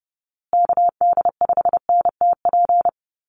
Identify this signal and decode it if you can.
Morse code, heard as a keyed tone, about 30 wpm, 700 Hz. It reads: KD5NTP
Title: USA Call Signs Repeated 2x 30wpm